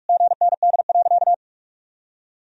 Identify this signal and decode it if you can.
Morse code, heard as a keyed tone, about 46 wpm, 700 Hz. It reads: GND.
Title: Sentences from Top 500 Words and Abbreviations 45wpm